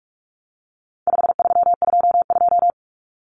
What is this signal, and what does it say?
Morse code, heard as a keyed tone, about 44 wpm, 700 Hz. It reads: 5322
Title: Common Cut Numbers - 4-Digit 45wpm